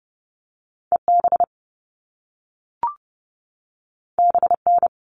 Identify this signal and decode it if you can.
Morse code, heard as a keyed tone, about 30 wpm, 700 Hz. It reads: EB BN
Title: All Cut Numbers - 2-Digit 30wpm